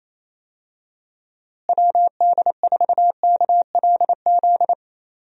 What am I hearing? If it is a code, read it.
Morse code, heard as a keyed tone, about 28 wpm, 700 Hz. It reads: WD4KLZ
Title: USA Call Signs - 6 Characters - Repeated 3x - 28wpm